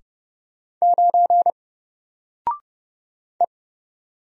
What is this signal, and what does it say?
Morse code, heard as a keyed tone, about 29 wpm, 700 Hz. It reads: 9 E
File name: Single-Letter-Number-30wpm